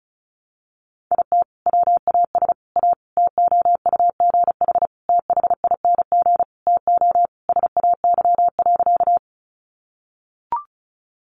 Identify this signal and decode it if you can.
Morse code, heard as a keyed tone, about 34 wpm, 700 Hz. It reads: IT WAS A TOUGH THING TO SAY.
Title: Sentences from Top 1200 Words 35wpm